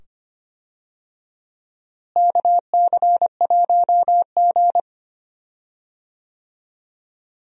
Morse code, heard as a keyed tone, about 25 wpm, 700 Hz. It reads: KC1G
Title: USA Call Signs - 4 Characters - Repeated 2x - 25wpm